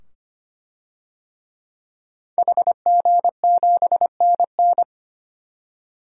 Morse code, heard as a keyed tone, about 25 wpm, 700 Hz. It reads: HG7NN
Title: DXCC-100 Call Signs - 5 Characters - 25wpm